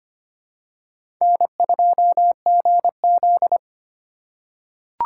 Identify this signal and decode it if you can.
Morse code, heard as a keyed tone, about 25 wpm, 700 Hz. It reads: N2GZ
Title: USA Call Signs - 4 Characters - 4x Character Spacing - 25wpm